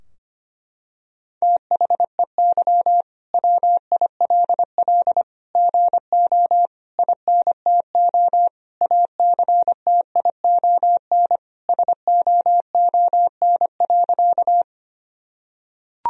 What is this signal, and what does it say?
Morse code, heard as a keyed tone, about 25 wpm, 700 Hz. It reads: THEY WILL GO INTO ACTION SOON.
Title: Sentences from Top 1000 Words - Review Part 4 - 25wpm